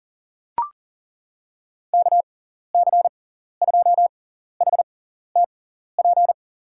Morse code, heard as a keyed tone, about 40 wpm, spaced as for 14 wpm, 700 Hz. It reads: KC2HTP
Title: USA Call Signs - 6 Characters - 6x Character Spacing - 40wpm